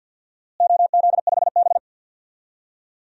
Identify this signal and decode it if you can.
Morse code, heard as a keyed tone, about 52 wpm, 700 Hz. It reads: OZ5B
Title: DXCC-100 Call Signs - 4 Characters - Repeated 3x - 50wpm